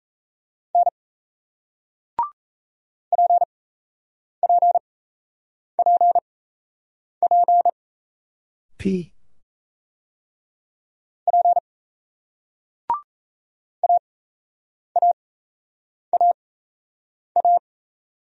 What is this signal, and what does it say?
Morse code, heard as a keyed tone, about 38 wpm, spaced as for 8 wpm, 700 Hz. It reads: N PPPP P AAAA